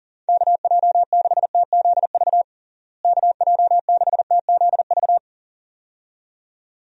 Morse code, heard as a keyed tone, about 40 wpm, 700 Hz. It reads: KJ6TZV KJ6TZV
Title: USA Call Signs - 6 Characters - Repeated 2x - 40wpm